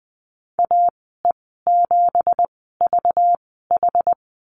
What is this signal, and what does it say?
Morse code, heard as a keyed tone, about 20 wpm, spaced as for 14 wpm, 700 Hz. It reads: AE7VH
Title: 193 Lesson 31g - Letter v - Call Signs 20wpm